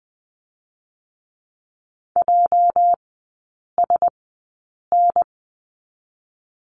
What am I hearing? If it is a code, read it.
Morse code, heard as a keyed tone, about 20 wpm, spaced as for 8 wpm, 700 Hz. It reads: JSN
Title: 205 Lesson 33e - Letter j - Three Characters 20wpm